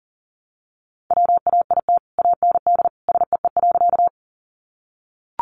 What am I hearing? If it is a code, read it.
Morse code, heard as a keyed tone, about 40 wpm, 700 Hz. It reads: WAIT AND SEE.
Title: Sentences from Top 500 Words 40wpm